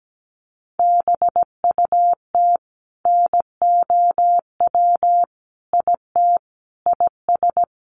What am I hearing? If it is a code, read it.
Morse code, heard as a keyed tone, about 17 wpm, 700 Hz. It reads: BUT NOW IT IS
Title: Sentences from Top 600 Words 17wpm